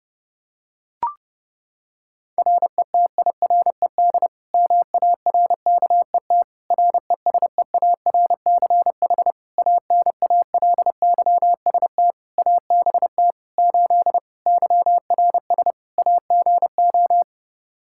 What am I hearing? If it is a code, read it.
Morse code, heard as a keyed tone, about 30 wpm, 700 Hz. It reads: RETIRED MARKET RESEARCH ANALYST ABT 8 YRS AGO
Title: Practice QSO - Exchange 3 - More Info - 30wpm